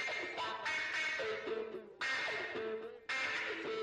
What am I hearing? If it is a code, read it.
Emergency Alert System (SAME) tones, heard as harsh buzzing bursts. It is heard from 0.7 to 3.4 s.
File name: wrct-log hi- 19